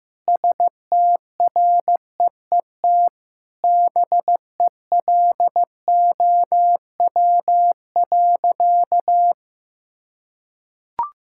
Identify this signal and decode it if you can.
Morse code, heard as a keyed tone, about 16 wpm, 700 Hz. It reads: STREET BELOW.
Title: Sentences from Top 1000 Words - Review Part 5 - 15wpm